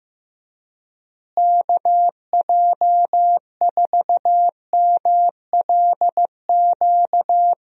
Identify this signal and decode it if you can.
Morse code, heard as a keyed tone, about 15 wpm, 700 Hz. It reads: KJ4MLQ
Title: USA Call Signs - 6 Characters - Repeated 3x - 15wpm